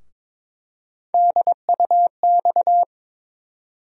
Morse code, heard as a keyed tone, about 22 wpm, 700 Hz. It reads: DUX
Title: USA Call Sign Postfixes - 3 Letters 22wpm